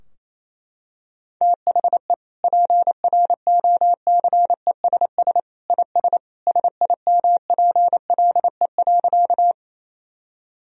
Morse code, heard as a keyed tone, about 28 wpm, 700 Hz. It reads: THE PROCESS IS SIMPLE.
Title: Sentences from Top 1000 Words - Review Part 3 - 28wpm